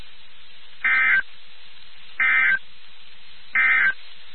Emergency Alert System (SAME) tones, heard as harsh buzzing bursts. It is heard from 0.7 to 4.0 s.
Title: NOAA Weather Radio WXK79